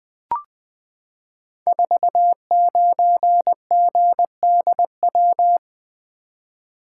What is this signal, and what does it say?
Morse code, heard as a keyed tone, about 20 wpm, 700 Hz. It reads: 49GDW
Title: Code Groups - 5 Letter:Numbers - 20wpm